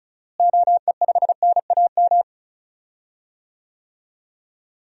Morse code, heard as a keyed tone, about 34 wpm, 700 Hz. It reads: OE5NAM
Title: DXCC-100 Call Signs - 6 Characters - Repeated 2x - 35wpm